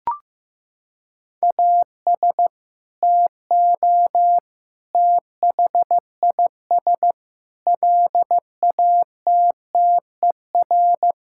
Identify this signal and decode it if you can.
Morse code, heard as a keyed tone, about 15 wpm, 700 Hz. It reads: AS TO THIS LATTER
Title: Sentences from Top 1600 Words 15wpm